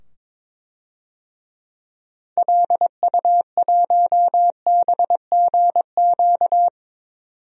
Morse code, heard as a keyed tone, about 21 wpm, 700 Hz. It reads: LU1BGQ
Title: DXCC-100 Call Signs - 6 Characters - 22wpm